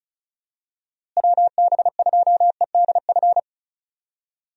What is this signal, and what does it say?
Morse code, heard as a keyed tone, about 34 wpm, 700 Hz. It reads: WB2EDF
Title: USA Call Signs - 6 Characters - Repeated 2x - 35wpm